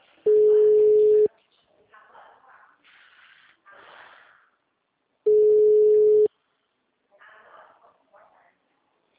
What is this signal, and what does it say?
Ringback tone (Europe, 425 Hz), with speech.